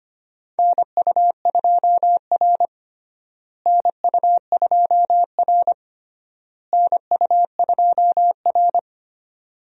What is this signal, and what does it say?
Morse code, heard as a keyed tone, about 25 wpm, 700 Hz. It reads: NU2R NU2R NU2R
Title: USA Call Signs - 4 Characters - Repeated 3x - 25wpm